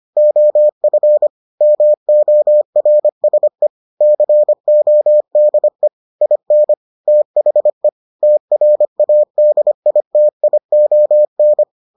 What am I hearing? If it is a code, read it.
Morse code, heard as a keyed tone, about 25 wpm, 600 Hz. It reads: OF MORSE CODE IN THE TRADITION